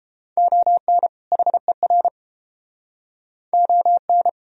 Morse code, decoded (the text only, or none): ON HER ON